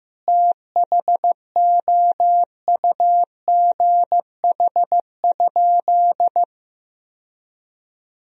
Morse code, heard as a keyed tone, about 15 wpm, 700 Hz. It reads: THOUGH?
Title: Sentences from Top 1000 Words - Review Part 2 - 15wpm